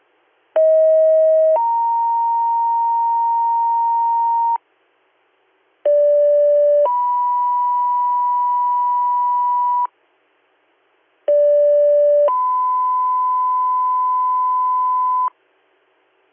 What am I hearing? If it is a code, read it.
Two-tone page, heard as a one-second tone, then a three-second tone, about 640 then 930 Hz, about 580 then 980 Hz, about 580 then 1010 Hz, with noise. It is heard from 0.6 to 15.3 s.